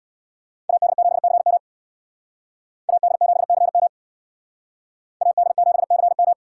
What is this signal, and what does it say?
Morse code, heard as a keyed tone, about 62 wpm, 700 Hz. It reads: AD7CM AD7CM AD7CM
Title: Speed Racing - USA 2x2 Call Signs 50wpm